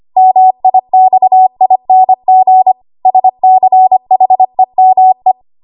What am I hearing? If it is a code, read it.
Morse code, heard as a keyed tone, about 25 wpm, 750 Hz. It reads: MIXING SCHEME
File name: Archive / 25 WPM / 25WPM